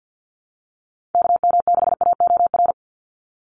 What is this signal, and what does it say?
Morse code, heard as a keyed tone, about 49 wpm, 700 Hz. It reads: KM6AOR
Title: USA Call Signs 50wpm